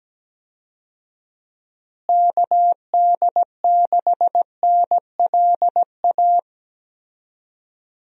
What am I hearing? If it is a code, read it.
Morse code, heard as a keyed tone, about 17 wpm, 700 Hz. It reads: KD6NLA
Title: USA Call Signs - 6 Characters - Repeated 2x - 17wpm